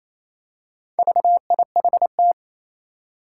Morse code, heard as a keyed tone, about 29 wpm, 700 Hz. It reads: VIHT
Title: Four Challenging Letter-Number Combinations 28wpm